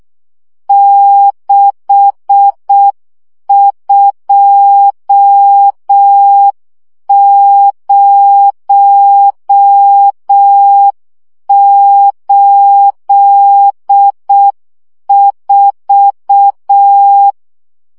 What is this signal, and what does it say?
Morse code, heard as a keyed tone, about 6 wpm, 800 Hz. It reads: 62084
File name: morze vizsga 2011-1 5